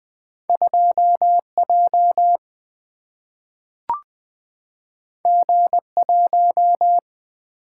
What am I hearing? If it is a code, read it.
Morse code, heard as a keyed tone, about 20 wpm, 700 Hz. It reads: 2J G1
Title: Single-Letter Single-Number 20wpm